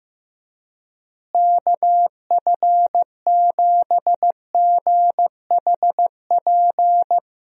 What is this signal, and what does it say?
Morse code, heard as a keyed tone, about 15 wpm, 700 Hz. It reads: KF7GHP
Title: USA Call Signs - 6 Characters - Repeated 2x - 15wpm